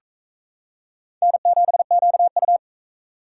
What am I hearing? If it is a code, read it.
Morse code, heard as a keyed tone, about 44 wpm, 700 Hz. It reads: N7QU